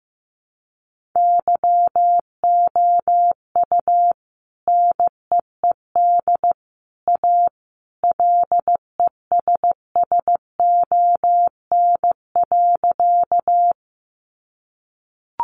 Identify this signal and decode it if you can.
Morse code, heard as a keyed tone, about 15 wpm, 700 Hz. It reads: YOU NEED A LESSON.
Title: Sentences from Top 1200 Words 15wpm